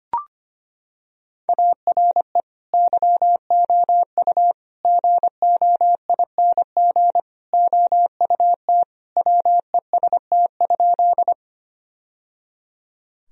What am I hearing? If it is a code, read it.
Morse code, heard as a keyed tone, about 25 wpm, 700 Hz. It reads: ARE YOU GOING OUT WEST?
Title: Sentences from Top 1000 Words - Review Part 2 - 25wpm